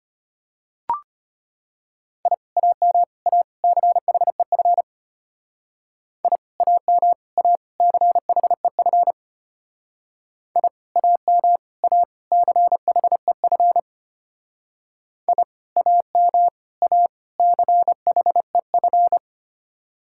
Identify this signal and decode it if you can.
Morse code, heard as a keyed tone, about 30 wpm, 700 Hz. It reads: I AM A CHEF I AM A CHEF I AM A CHEF I AM A CHEF
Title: Speed-Racing Common QSO Phrases Encore - 25wpm